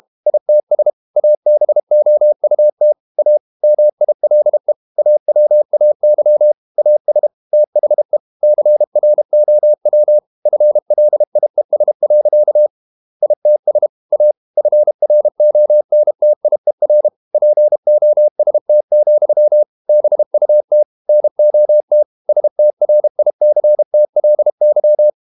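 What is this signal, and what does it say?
Morse code, heard as a keyed tone, about 32 wpm, 600 Hz. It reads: ITS ABOUT A MILE AWAY AS THE CROW FLIES. ITS A FRONTIER POST, BUT NOT STRICTLY
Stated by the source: Morse code practice at 32 wpm from CW med Gnister, reading Ken Follett's novel Never in English.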